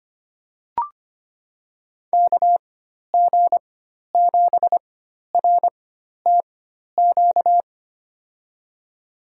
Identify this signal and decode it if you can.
Morse code, heard as a keyed tone, about 25 wpm, spaced as for 12 wpm, 700 Hz. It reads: KG7RTQ
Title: USA Call Signs - 6 Characters - 4x Character Spacing - 25wpm